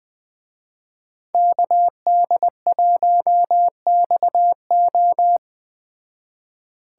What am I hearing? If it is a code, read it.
Morse code, heard as a keyed tone, about 19 wpm, 700 Hz. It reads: KD1XO
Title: Speed Racing - USA 2x2 Call Signs 17wpm